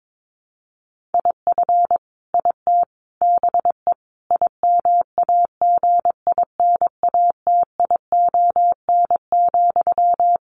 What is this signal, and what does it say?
Morse code, heard as a keyed tone, about 22 wpm, 700 Hz. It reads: IF IT BE IMAGINATION,
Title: Sentences from Top 1300 Words 22wpm